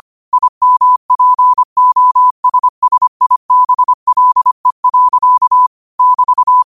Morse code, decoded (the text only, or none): IMPOSSIBLE. <BT>